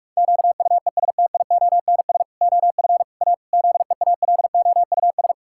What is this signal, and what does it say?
Morse code, heard as a keyed tone, about 46 wpm, 700 Hz. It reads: QUESTIONS OF A ZEALOUS
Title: Pangram Challenge v2 - 45wpm